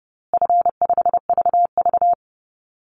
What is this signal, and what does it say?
Morse code, heard as a keyed tone, about 30 wpm, 700 Hz. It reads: F5VV
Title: Easy Call Signs Repeated 3x - 30wpm